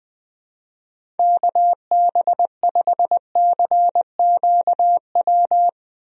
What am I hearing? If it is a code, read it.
Morse code, heard as a keyed tone, about 20 wpm, 700 Hz. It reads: KB5CQW
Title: USA Call Signs - 6 Characters - Repeated 2x - 20wpm